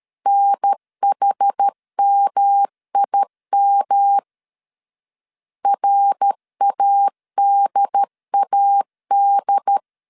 Morse code, heard as a keyed tone, about 13 wpm, 800 Hz. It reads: NHMIM RADAD
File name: CW / DET / 1 3 1500C S B 12wpm 2828